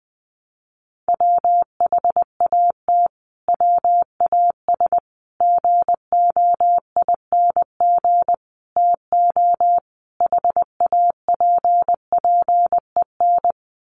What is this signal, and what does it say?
Morse code, heard as a keyed tone, about 20 wpm, 700 Hz. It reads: WHAT WAS GOING TO HAPPEN
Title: Sets of 5 Words Encore 20wpm